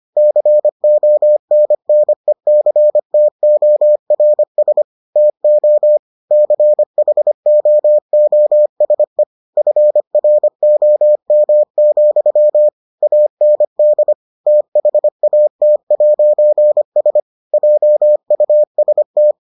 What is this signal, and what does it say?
Morse code, heard as a keyed tone, about 25 wpm, 600 Hz. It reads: CONNECTORS TO CHOOSE FROM, AND THAT'S JUST